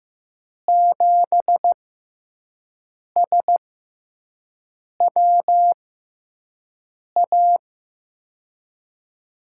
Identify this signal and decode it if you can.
Morse code, heard as a keyed tone, about 15 wpm, spaced as for 5 wpm, 700 Hz. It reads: 7SWA
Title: USA Call Signs - 6x Character Spacing - 15wpm